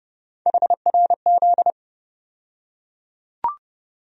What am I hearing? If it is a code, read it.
Morse code, heard as a keyed tone, about 30 wpm, 700 Hz. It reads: HRZ
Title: USA Call Sign Postfixes - 3 Letters 30wpm